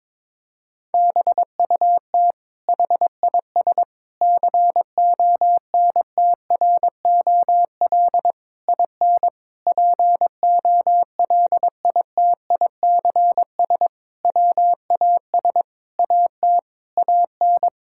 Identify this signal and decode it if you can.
Morse code, heard as a keyed tone, about 22 wpm, 700 Hz. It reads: BUT HIS CONTROL IN POLITICS WAS AT AN
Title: Sentences from Top 1000 Words - Review Part 6 - 22wpm